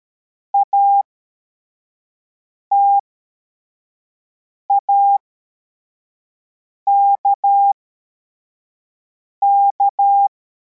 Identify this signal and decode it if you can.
Morse code, heard as a keyed tone, about 13 wpm, spaced as for 4 wpm, 800 Hz. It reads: ATAKK